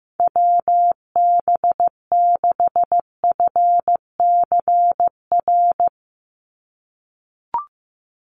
Morse code, heard as a keyed tone, about 15 wpm, 700 Hz. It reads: WB6FCR
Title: USA Call Signs 15wpm